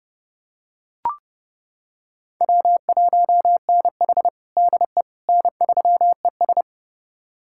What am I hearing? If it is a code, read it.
Morse code, heard as a keyed tone, about 30 wpm, 700 Hz. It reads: W1NH DE N3ES
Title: USA Callsign DE Callsign - 4 Characters - 30wpm